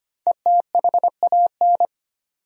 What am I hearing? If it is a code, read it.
Morse code, heard as a keyed tone, about 25 wpm, 700 Hz. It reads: ETHAN
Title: USA Common Names - 25wpm